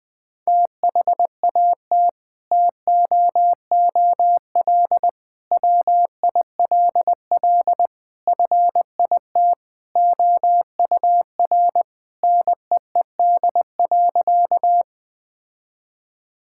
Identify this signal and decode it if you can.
Morse code, heard as a keyed tone, about 20 wpm, 700 Hz. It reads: THAT TOOL WILL FIT OUR NEED.
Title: Sentences from Top 1000 Words - Review Part 6 - 20wpm